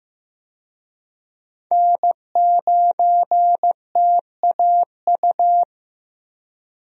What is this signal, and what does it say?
Morse code, heard as a keyed tone, about 15 wpm, 700 Hz. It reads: N9TAU
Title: USA Call Signs - 5 Characters - Repeated 2x - 15wpm